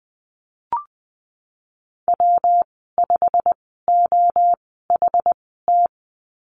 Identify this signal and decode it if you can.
Morse code, heard as a keyed tone, about 19 wpm, spaced as for 14 wpm, 700 Hz. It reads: W5OHT
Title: 186 Lesson 30g - Letter b - Call Signs 20wpm